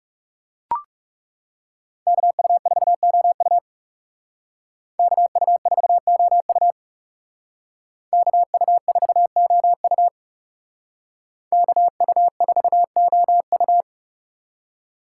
Morse code, heard as a keyed tone, about 36 wpm, 700 Hz. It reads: KU4OU KU4OU KU4OU KU4OU
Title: Speed Racing - USA 2x2 Call Signs 30wpm